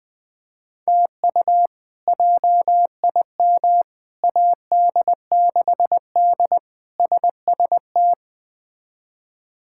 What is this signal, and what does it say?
Morse code, heard as a keyed tone, about 20 wpm, 700 Hz. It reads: TU JIM AD6D SST
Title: K1USN SST QSO Practice - 20wpm